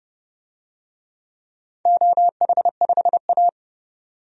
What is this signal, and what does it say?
Morse code, heard as a keyed tone, about 32 wpm, 700 Hz. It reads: OH5A
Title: DXCC-100 Call Signs - 4 Characters - 30wpm